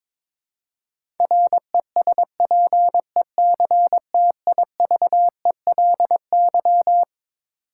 Morse code, heard as a keyed tone, about 22 wpm, 700 Hz. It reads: RESPECTIVELY
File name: Twelve Letter Words - 22wpm